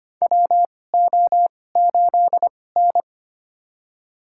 Morse code, heard as a keyed tone, about 25 wpm, spaced as for 18 wpm, 700 Hz. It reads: WO8N
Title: USA Call Signs - 4 Characters - 2x Character Spacing - 25wpm